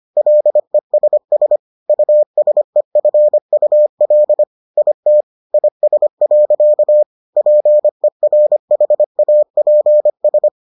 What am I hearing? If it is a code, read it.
Morse code, heard as a keyed tone, about 25 wpm, 600 Hz. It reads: LESS USEFUL IT IS. PERHAPS